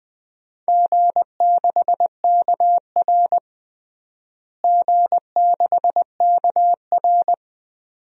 Morse code, heard as a keyed tone, about 20 wpm, 700 Hz. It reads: G6KR G6KR
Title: DXCC-100 Call Signs - 4 Characters - Repeated 2x - 20wpm